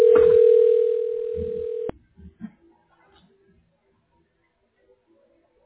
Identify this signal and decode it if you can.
Ringback tone (North America), 440 + 480 Hz.